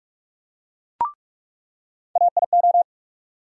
Morse code, heard as a keyed tone, about 42 wpm, 700 Hz. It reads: AIO
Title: USA Call Sign Postfixes - 3 Letters 45wpm